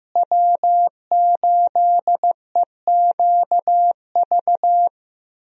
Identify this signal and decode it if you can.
Morse code, heard as a keyed tone, about 15 wpm, 700 Hz. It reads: W8EQV
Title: USA Call Signs - 5 Characters - Repeated 3x - 15wpm